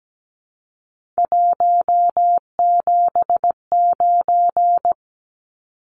Morse code, heard as a keyed tone, about 17 wpm, 700 Hz. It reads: 179
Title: Numbers - 3-Digit 17wpm